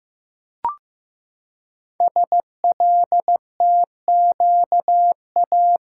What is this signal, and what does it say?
Morse code, heard as a keyed tone, about 15 wpm, 700 Hz. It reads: SLTQA
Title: Code Groups - 5 Letters - 15wpm